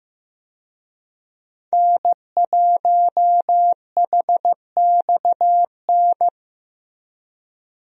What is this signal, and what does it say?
Morse code, heard as a keyed tone, about 15 wpm, 700 Hz. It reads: N1HXN
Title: USA Call Signs - 5 Characters - Repeated 3x - 15wpm